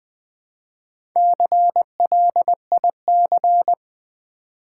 Morse code, heard as a keyed tone, about 20 wpm, 700 Hz. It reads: CLIC
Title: Build A Word - 1 Letter At A Time - 5-Letter Words - 20wpm